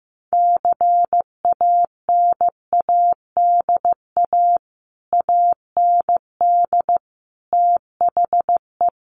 Morse code, heard as a keyed tone, about 15 wpm, 700 Hz. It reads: CANADA AND THE
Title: Sets of 5 Words Encore 15wpm